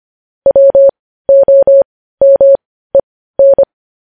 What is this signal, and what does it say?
Morse code, heard as a keyed tone, about 25 wpm, spaced as for 15 wpm, 550 Hz. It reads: WOMEN
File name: podcast / News headlines-1x-25x15